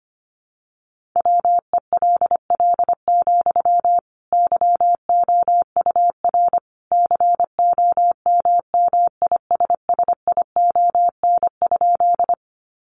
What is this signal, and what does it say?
Morse code, heard as a keyed tone, about 25 wpm, 700 Hz. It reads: WELL, YOUR COMMISSION?
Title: Sentences from Top 1400 Words 25wpm